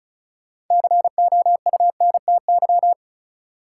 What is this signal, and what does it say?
Morse code, heard as a keyed tone, about 36 wpm, 700 Hz. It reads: COUNTY
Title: Sentences from Top 1200 Words - New Word Drill - 35wpm